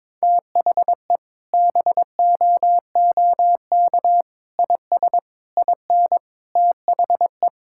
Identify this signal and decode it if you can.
Morse code, heard as a keyed tone, about 22 wpm, 700 Hz. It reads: THE BOOK IS IN THE
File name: practice / 22wpm / Sentences with 3-Word Phrases - 22wpm